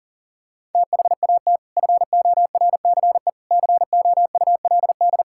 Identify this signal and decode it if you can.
Morse code, heard as a keyed tone, about 40 wpm, 700 Hz. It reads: THAT FORCE COULD
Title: Sentences from Top 1000 Words - Review Part 5 - 40wpm